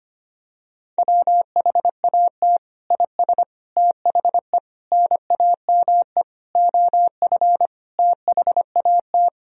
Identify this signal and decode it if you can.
Morse code, heard as a keyed tone, about 25 wpm, 700 Hz. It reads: WHAT IS THE NAME OF THAT
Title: Sentences from Top 1000 Words - Review Part 1 - 25wpm